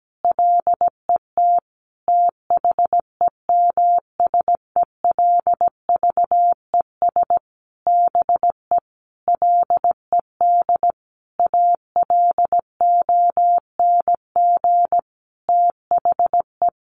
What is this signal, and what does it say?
Morse code, heard as a keyed tone, about 17 wpm, 700 Hz. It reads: LET THEMSELVES BE LED ALONG THE
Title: A Sound Of Thunder 17wpm